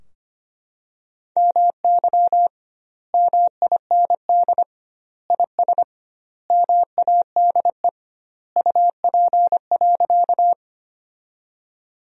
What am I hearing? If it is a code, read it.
Morse code, heard as a keyed tone, about 25 wpm, 700 Hz. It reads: MY MIND IS MADE UP.
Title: 2x Word-Spacing - Sentences from Top 500 Words - 25wpm